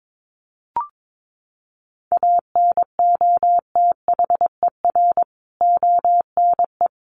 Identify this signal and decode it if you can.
Morse code, heard as a keyed tone, about 22 wpm, 700 Hz. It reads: ANOTHER ONE
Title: Additional Easy Phrases 22wpm